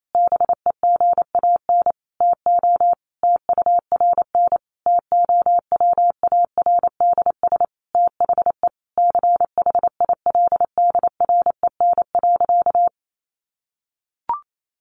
Morse code, heard as a keyed tone, about 28 wpm, 700 Hz. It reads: BEGAN TO TURN TOWARDS THE CHILDREN.
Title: Sentences from Top 1100 Words 28wpm